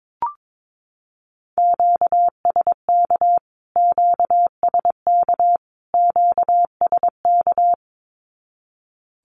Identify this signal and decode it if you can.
Morse code, heard as a keyed tone, about 22 wpm, 700 Hz. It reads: QSK QSK QSK
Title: Q-Codes Repeated 3x 22wpm